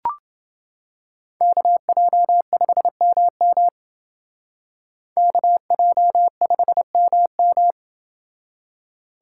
Morse code, heard as a keyed tone, about 29 wpm, 700 Hz. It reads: KJ5MM KJ5MM